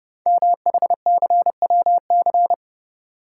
Morse code, heard as a keyed tone, about 30 wpm, 700 Hz. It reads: MHCWC
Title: Code Groups - 5 Letter:Numbers - 30wpm